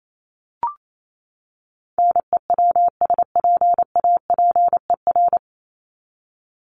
Morse code, heard as a keyed tone, about 28 wpm, 700 Hz. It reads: NEWSPAPER
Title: Top 751-1000 Words 28wpm